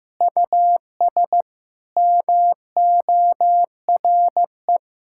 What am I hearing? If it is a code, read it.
Morse code, heard as a keyed tone, about 16 wpm, 700 Hz. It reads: US MORE
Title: Sentences from Top 1600 Words 15wpm